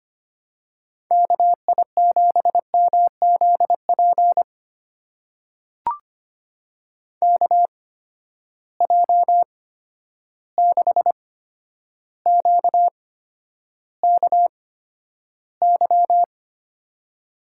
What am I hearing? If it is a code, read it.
Morse code, heard as a keyed tone, about 25 wpm, spaced as for 11 wpm, 700 Hz. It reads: KI7MZP KJ6QKY
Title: USA Call Signs - 8x Character Spacing - 25wpm